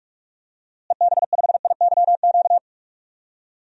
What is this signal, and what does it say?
Morse code, heard as a keyed tone, about 46 wpm, 700 Hz. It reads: EB5IYQ
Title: DXCC-100 Call Signs - 6 Characters - Repeated 2x - 45wpm